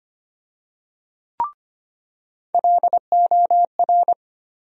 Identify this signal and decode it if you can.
Morse code, heard as a keyed tone, about 25 wpm, 700 Hz. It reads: LOR
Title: USA Call Sign Postfixes - 3 Letters 25wpm